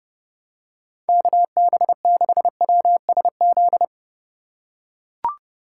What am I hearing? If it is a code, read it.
Morse code, heard as a keyed tone, about 30 wpm, 700 Hz. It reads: KB6WSZ
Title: USA Call Signs - 6 Characters - 8x Character Spacing - 30wpm